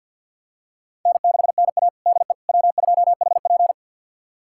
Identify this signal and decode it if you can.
Morse code, heard as a keyed tone, about 52 wpm, 700 Hz. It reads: N6NA DE W2HP
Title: USA Callsign DE Callsign - 4 Characters - 50wpm